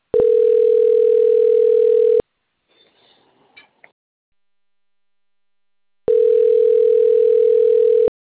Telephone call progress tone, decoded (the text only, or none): ringback tone